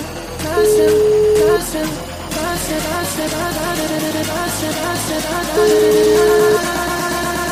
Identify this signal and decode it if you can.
Ringback tone (Europe, 425 Hz), with speech.